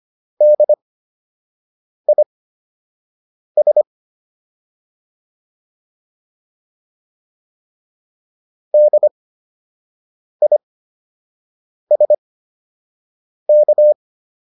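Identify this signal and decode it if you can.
Morse code, heard as a keyed tone, about 25 wpm, spaced as for 6 wpm, 600 Hz. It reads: DIS DISK
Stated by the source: CWops Fundamental Class Morse practice material.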